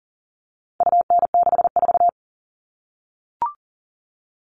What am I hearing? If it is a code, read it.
Morse code, heard as a keyed tone, about 40 wpm, 700 Hz. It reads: UN64
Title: All Cut Numbers - 4-Digit 40wpm